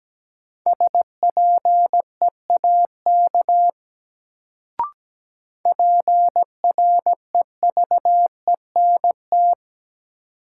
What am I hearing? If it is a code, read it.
Morse code, heard as a keyed tone, about 17 wpm, 700 Hz. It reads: SPEAK PREVENT
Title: Sentences from Top 1100 Words - New Word Drill Alt - 17wpm